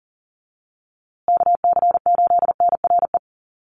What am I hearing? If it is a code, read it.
Morse code, heard as a keyed tone, about 40 wpm, 700 Hz. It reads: KC8NRE
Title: USA Call Signs Repeated 3x 40wpm